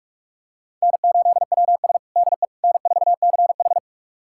Morse code, heard as a keyed tone, about 41 wpm, 700 Hz. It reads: N8WS DE N4CH
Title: USA Callsign DE Callsign - 4 Characters - 45wpm